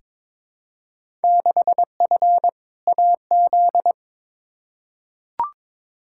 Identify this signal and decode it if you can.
Morse code, heard as a keyed tone, about 22 wpm, 700 Hz. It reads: 6F AZ
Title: Field Day Exchanges Encore - 22wpm